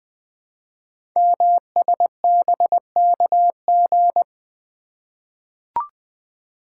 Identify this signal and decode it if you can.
Morse code, heard as a keyed tone, about 20 wpm, 700 Hz. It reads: MSBKG
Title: Code Groups - 5 Letters - 20wpm